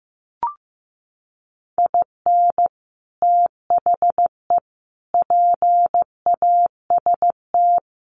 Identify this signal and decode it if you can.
Morse code, heard as a keyed tone, about 15 wpm, 700 Hz. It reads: IN THE PAST
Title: Sets of 5 Words 15wpm